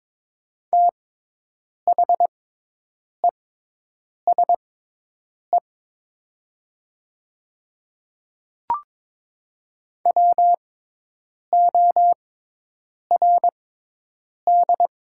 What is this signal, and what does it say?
Morse code, heard as a keyed tone, about 22 wpm, spaced as for 8 wpm, 700 Hz. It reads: THESE WORD
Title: 6x Character Spacing - Top 100 Words - 22wpm